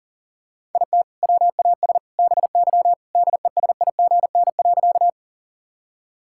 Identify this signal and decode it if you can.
Morse code, heard as a keyed tone, about 40 wpm, 700 Hz. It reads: IT WAS BY DESIGN.
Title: Sentences from Top 1000 Words - Review Part 5 - 40wpm